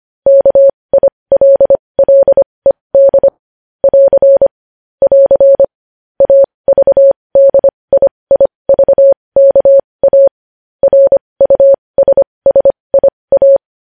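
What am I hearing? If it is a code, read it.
Morse code, heard as a keyed tone, about 25 wpm, spaced as for 20 wpm, 550 Hz. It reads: KILLED <AR> <AR> AVDIIVKA RUSSIA
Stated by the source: Morse practice sending news headlines at 20 wpm.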